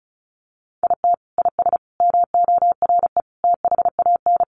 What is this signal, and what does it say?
Morse code, heard as a keyed tone, about 34 wpm, 700 Hz. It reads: IT IS MORE THAN
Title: Sentences from Top 1400 Words 35wpm